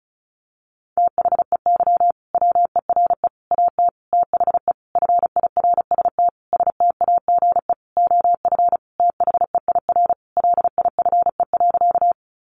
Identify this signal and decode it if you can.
Morse code, heard as a keyed tone, about 34 wpm, 700 Hz. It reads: THEY WERE AT THE FIRST STAGE OF THEIR LIFE.
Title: Sentences from Top 1200 Words 35wpm